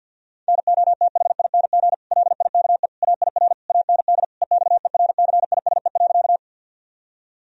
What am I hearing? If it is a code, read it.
Morse code, heard as a keyed tone, about 50 wpm, 700 Hz. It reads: NOTHING LIKE AIR AND EXERCISE.
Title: Sentences from Top 1000 Words - Review Part 4 - 50wpm